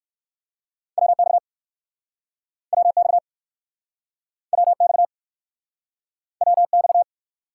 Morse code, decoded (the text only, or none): WX WX WX WX